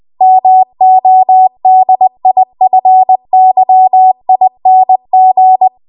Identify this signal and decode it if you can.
Morse code, heard as a keyed tone, about 20 wpm, 750 Hz. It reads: MODIFYING
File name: Archive / 20 WPM / 20WPM